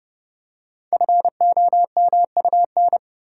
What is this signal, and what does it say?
Morse code, heard as a keyed tone, about 30 wpm, 700 Hz. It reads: FOMUN
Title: Code Groups - 5 Letters - 30wpm